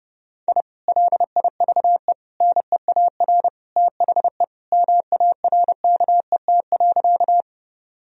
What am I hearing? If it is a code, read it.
Morse code, heard as a keyed tone, about 30 wpm, 700 Hz. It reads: I LIVE NEAR THE MARKET.
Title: Sentences from Top 1000 Words - Review Part 3 - 30wpm